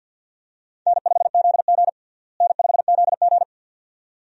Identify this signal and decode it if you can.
Morse code, heard as a keyed tone, about 53 wpm, 700 Hz. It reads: N5ZG N5ZG
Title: USA Call Signs - 4 Characters - Repeated 2x - 50wpm